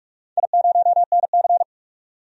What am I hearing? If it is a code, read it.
Morse code, heard as a keyed tone, about 44 wpm, 700 Hz. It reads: I0NC
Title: DXCC-100 Call Signs - 4 Characters - 45wpm